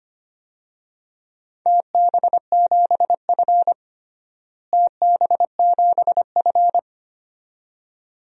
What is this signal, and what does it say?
Morse code, heard as a keyed tone, about 25 wpm, 700 Hz. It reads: TB7F TB7F
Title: DXCC-100 Call Signs - 4 Characters - Repeated 2x - 25wpm